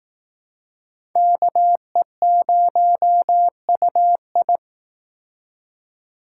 Morse code, heard as a keyed tone, about 18 wpm, 700 Hz. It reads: KE0UI
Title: Speed Racing - USA Call Signs 15wpm